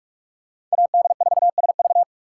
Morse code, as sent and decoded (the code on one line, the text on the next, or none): .- -.. ....- ... ...-
AD4SV